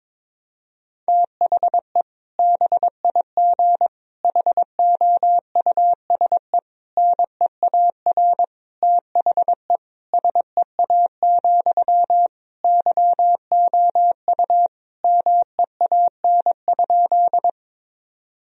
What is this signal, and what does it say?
Morse code, heard as a keyed tone, about 22 wpm, 700 Hz. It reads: THE BIG HOUSE NEAR THE SEA, YOU MEAN?
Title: Sentences from Top 1000 Words - Review Part 1 - 22wpm